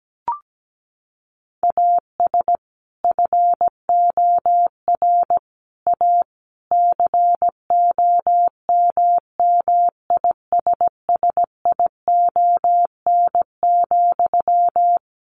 Morse code, heard as a keyed tone, about 17 wpm, 700 Hz. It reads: AS FOR A COMMISSION,
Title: Sentences from Top 1400 Words 17wpm